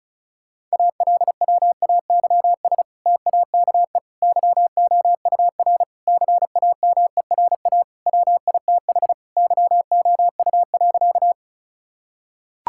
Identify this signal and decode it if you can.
Morse code, heard as a keyed tone, about 36 wpm, 700 Hz. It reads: ALWAYS TAKE YOUR CAMERA WITH YOU.
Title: Sentences from Top 1000 Words - Review Part 5 - 35wpm